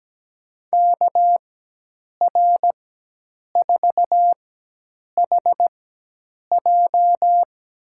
Morse code, heard as a keyed tone, about 17 wpm, spaced as for 8 wpm, 700 Hz. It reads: KR4HJ
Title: USA Call Signs - 5 Characters - 4x Character Spacing - 17wpm